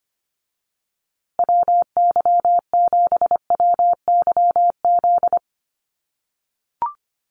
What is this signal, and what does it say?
Morse code, heard as a keyed tone, about 25 wpm, 700 Hz. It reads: WY7WYZ
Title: USA Call Signs Repeated 2x 25wpm